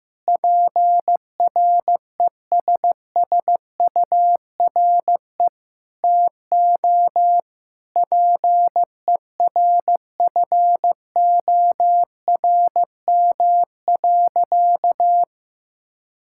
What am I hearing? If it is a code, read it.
Morse code, heard as a keyed tone, about 16 wpm, 700 Hz. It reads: PRESSURE TO PERFORM.
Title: Sentences from Top 1000 Words - Review Part 6 - 15wpm